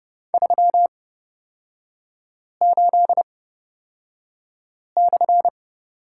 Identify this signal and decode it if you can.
Morse code, heard as a keyed tone, about 30 wpm, spaced as for 5 wpm, 700 Hz. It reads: 38/
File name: Lesson 38e - Letter q - Three Characters 30wpm